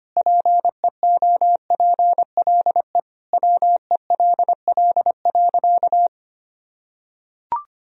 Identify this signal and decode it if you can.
Morse code, heard as a keyed tone, about 25 wpm, 700 Hz. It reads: PEOPLE WELL.
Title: Sentences from Top 1000 Words - Review Part 2 - 25wpm